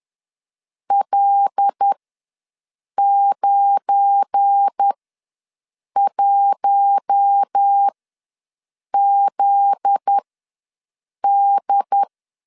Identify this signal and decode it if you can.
Morse code, heard as a keyed tone, about 10 wpm, spaced as for 6 wpm, 800 Hz. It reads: L91ZD
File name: LEZIONI / 8 YJ45 / 8 YJ4 ws 10wpm 4020